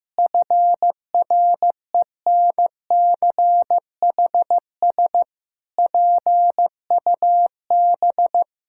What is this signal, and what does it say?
Morse code, heard as a keyed tone, about 15 wpm, 700 Hz. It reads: FRENCHS PUB
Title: Pangram Challenge v2 - 15wpm